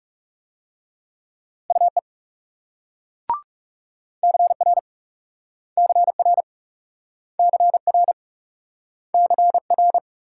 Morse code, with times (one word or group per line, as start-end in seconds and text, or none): 1.7-2.0 UE
4.2-4.8 CR
5.8-6.4 CR
7.4-8.1 CR
9.1-10.0 CR